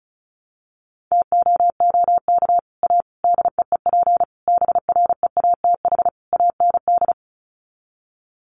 Morse code, heard as a keyed tone, about 34 wpm, 700 Hz. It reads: TOOK A DEEP BREATH AND
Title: Sets of 5 Words Encore 35wpm